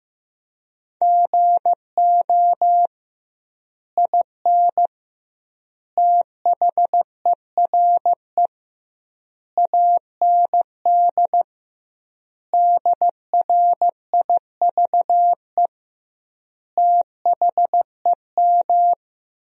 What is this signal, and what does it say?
Morse code, heard as a keyed tone, about 15 wpm, 700 Hz. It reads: GO IN THERE AND DRIVE THEM
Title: 2x Word-Spacing - Sentences from Top 500 Words - 15wpm